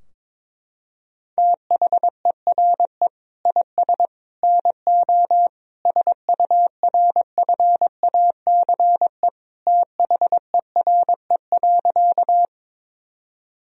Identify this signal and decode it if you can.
Morse code, heard as a keyed tone, about 22 wpm, 700 Hz. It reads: THERE IS NO SURFACE THERE.
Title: Sentences from Top 1000 Words - Review Part 3 - 22wpm